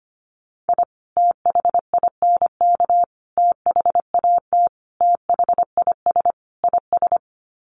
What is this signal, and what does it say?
Morse code, heard as a keyed tone, about 25 wpm, 700 Hz. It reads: I THINK THAT THIS IS
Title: Sets of 5 Words Encore2 25wpm